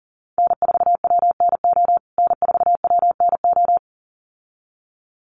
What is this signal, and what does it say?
Morse code, heard as a keyed tone, about 40 wpm, 700 Hz. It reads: N4WNO N4WNO
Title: USA Call Signs Repeated 2x 40wpm